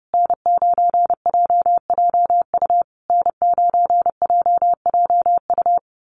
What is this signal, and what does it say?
Morse code, heard as a keyed tone, about 30 wpm, 700 Hz. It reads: N9JJU N9JJU
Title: USA Call Signs Repeated 3x 30wpm